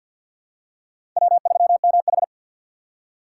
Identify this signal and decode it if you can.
Morse code, heard as a keyed tone, about 52 wpm, 700 Hz. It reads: W3MH
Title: USA Call Signs - 4 Characters - 6x Character Spacing - 50wpm